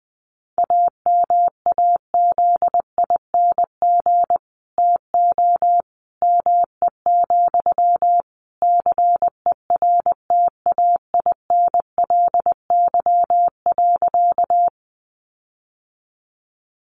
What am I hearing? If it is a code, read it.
Morse code, heard as a keyed tone, about 20 wpm, 700 Hz. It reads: AMAZING TO ME, CERTAINLY.
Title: Sentences from Top 1500 Words 20wpm